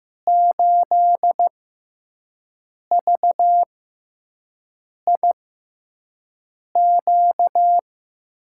Morse code, heard as a keyed tone, about 15 wpm, spaced as for 5 wpm, 700 Hz. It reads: 8VIQ
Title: USA Call Signs - 6 Characters - 6x Character Spacing - 15wpm